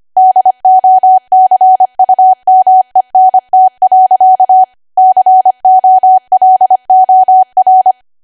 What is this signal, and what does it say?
Morse code, heard as a keyed tone, about 25 wpm, 750 Hz. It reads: DOCUMENT. COLOR
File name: Archive / 25 WPM / 25WPM